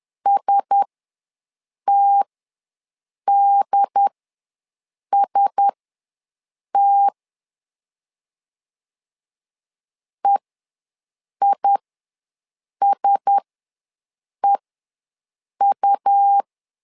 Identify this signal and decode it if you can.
Morse code, heard as a keyed tone, about 10 wpm, spaced as for 6 wpm, 800 Hz. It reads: STDST EISEU
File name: LEZIONI / 4 Q1S ws 10wpm 2208